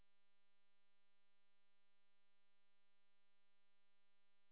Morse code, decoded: none